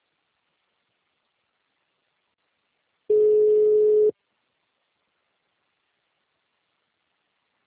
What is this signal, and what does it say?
Ringback tone (Europe, 425 Hz).
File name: out- -2014- - - .41800